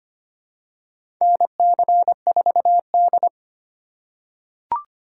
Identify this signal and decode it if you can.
Morse code, heard as a keyed tone, about 25 wpm, 700 Hz. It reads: NC4D
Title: USA Call Signs - 4 Characters - 2x Character Spacing - 25wpm